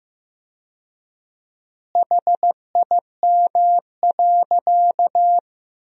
Morse code, heard as a keyed tone, about 15 wpm, 700 Hz. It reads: HIM.
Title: 4x Word-Spacing - Sentences from Top 100 Words - 15wpm